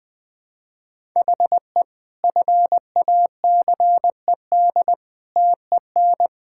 Morse code, heard as a keyed tone, about 20 wpm, 700 Hz. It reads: HE FACED TEN
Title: Sentences with 9-Word Phrases - Spoken Beginning - 20wpm